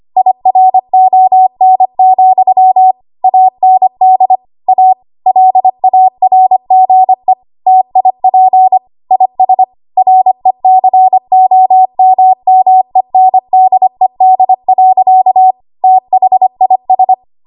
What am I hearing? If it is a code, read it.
Morse code, heard as a keyed tone, about 25 wpm, 750 Hz. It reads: IRON, AND A LARGE TIP IS RECOMMENDED. THIS
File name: Archive / 25 WPM / 25WPM